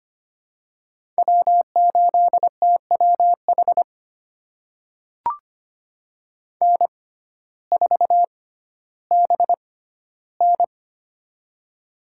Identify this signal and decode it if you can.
Morse code, heard as a keyed tone, about 25 wpm, spaced as for 14 wpm, 700 Hz. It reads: W8TWH N4BN
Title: USA Call Signs - 6x Character Spacing - 25wpm